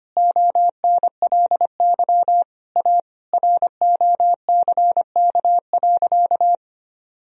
Morse code, heard as a keyed tone, about 25 wpm, 700 Hz. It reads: ONLY A ROCK.
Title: Sentences from Top 1000 Words - Review Part 2 - 25wpm